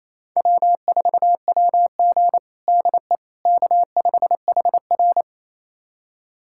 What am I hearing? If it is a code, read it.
Morse code, heard as a keyed tone, about 28 wpm, 700 Hz. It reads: W4WG DE K5HR
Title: USA Callsign DE Callsign - 4 Characters - 28wpm